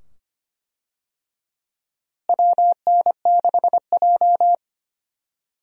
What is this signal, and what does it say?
Morse code, heard as a keyed tone, about 25 wpm, 700 Hz. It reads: WN6J
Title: USA Call Signs - 4 Characters - Repeated 2x - 25wpm